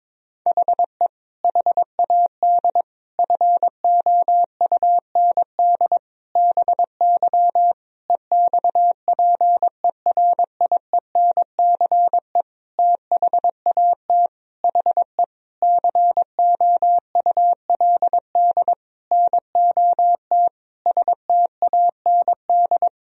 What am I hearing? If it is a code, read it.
Morse code, heard as a keyed tone, about 22 wpm, 700 Hz. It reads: HE HAD FOUND BY EXPERIENCE THAT HE COULD NOT STAND
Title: Sentences from Top 1000 Words - Review Part 3 - 22wpm